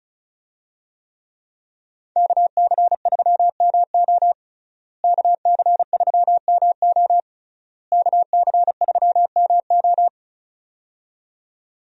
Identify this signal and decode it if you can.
Morse code, heard as a keyed tone, about 36 wpm, 700 Hz. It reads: KC3MO KC3MO KC3MO
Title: USA Call Signs - 5 Characters - Repeated 3x - 35wpm